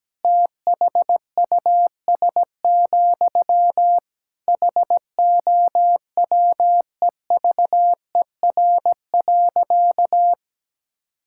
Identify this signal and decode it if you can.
Morse code, heard as a keyed tone, about 17 wpm, 700 Hz. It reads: THUS, HOWEVER.
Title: Sentences from Top 1000 Words - Review Part 5 - 17wpm